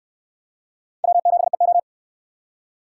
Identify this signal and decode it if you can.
Morse code, heard as a keyed tone, about 68 wpm, 700 Hz. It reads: W7EO